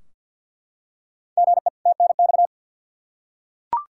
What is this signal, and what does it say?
Morse code, heard as a keyed tone, about 51 wpm, 700 Hz. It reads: GE TNX
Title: Practice QSO - Exchange 1 - Beginning - 50wpm